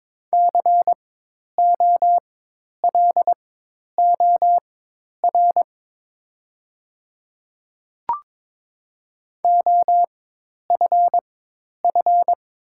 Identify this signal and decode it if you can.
Morse code, heard as a keyed tone, about 22 wpm, spaced as for 10 wpm, 700 Hz. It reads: COLOR OFF
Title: 4x Character Spacing - Top 300 Words - 22wpm